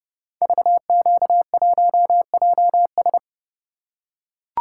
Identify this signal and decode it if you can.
Morse code, heard as a keyed tone, about 30 wpm, 700 Hz. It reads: VQ1JS
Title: Code Groups - 5 Letter:Numbers - 30wpm